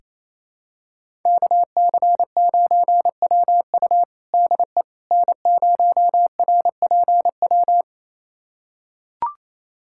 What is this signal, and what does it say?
Morse code, heard as a keyed tone, about 28 wpm, 700 Hz. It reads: KC9WU DE N0RPW
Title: USA Callsign DE Callsign - 5 Characters - 28wpm